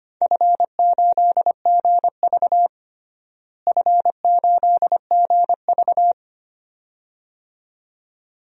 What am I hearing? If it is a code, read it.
Morse code, heard as a keyed tone, about 25 wpm, 700 Hz. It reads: F8GV F8GV
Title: DXCC-100 Call Signs - 4 Characters - Repeated 2x - 25wpm